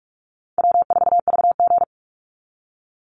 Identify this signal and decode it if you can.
Morse code, heard as a keyed tone, about 44 wpm, 700 Hz. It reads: W4VG
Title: Easy Call Signs Repeated 3x - 45wpm